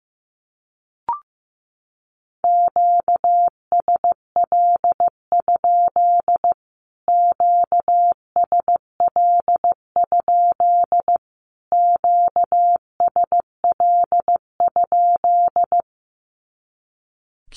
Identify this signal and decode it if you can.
Morse code, heard as a keyed tone, about 15 wpm, 700 Hz. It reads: QSL? QSL? QSL?
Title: Q-Codes Repeated 3x 15wpm